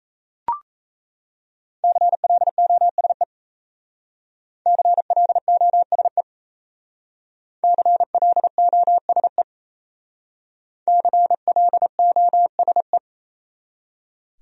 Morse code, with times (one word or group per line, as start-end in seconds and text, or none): 1.8-3.2 CLOSE
4.7-6.2 CLOSE
7.6-9.4 CLOSE
10.9-13.0 CLOSE